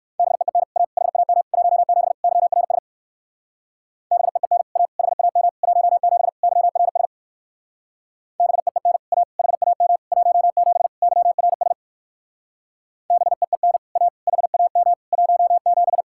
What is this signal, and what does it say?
Morse code, heard as a keyed tone, about 56 wpm, 700 Hz. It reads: BEEN A HAM 17 YRS BEEN A HAM 17 YRS BEEN A HAM 17 YRS BEEN A HAM 17
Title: Speed-Racing Common QSO Phrases - 45wpm